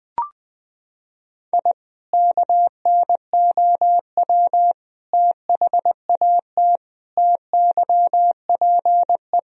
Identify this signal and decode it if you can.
Morse code, heard as a keyed tone, about 20 wpm, 700 Hz. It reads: I KNOW THAT TYPE
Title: Sentences from Top 1000 Words - Review Part 3 - 20wpm